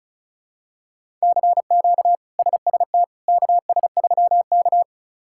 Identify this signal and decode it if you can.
Morse code, heard as a keyed tone, about 36 wpm, 700 Hz. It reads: CQ SST KS3K
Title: K1USN SST QSO Practice - 35wpm